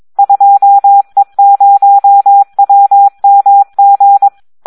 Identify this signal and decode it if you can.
Morse code, heard as a keyed tone, about 22 wpm, 800 Hz. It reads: 2E0WMG
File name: Audio practica CW / SD 1 22